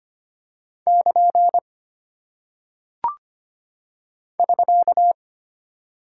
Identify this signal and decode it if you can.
Morse code, heard as a keyed tone, about 25 wpm, 700 Hz. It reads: <KN> <SK>
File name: Single Punctuation-Prosign - 25wpm